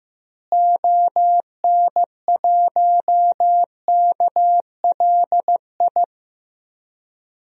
Morse code, heard as a keyed tone, about 15 wpm, 700 Hz. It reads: ON1KLI
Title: DXCC-100 Call Signs - 6 Characters - Repeated 3x - 15wpm